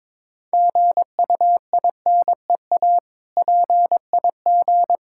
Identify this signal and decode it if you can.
Morse code, heard as a keyed tone, about 22 wpm, 690 Hz. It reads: GUINEA PIG
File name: Difficult Words with Hints - 22wpm